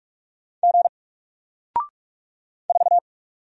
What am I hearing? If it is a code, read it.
Morse code, heard as a keyed tone, about 44 wpm, 700 Hz. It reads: G 4
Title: Single Letter-Number Alt 45wpm